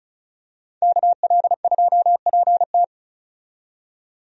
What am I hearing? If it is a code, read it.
Morse code, heard as a keyed tone, about 34 wpm, 700 Hz. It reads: KL2PT
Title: USA Call Signs - 5 Characters - 35wpm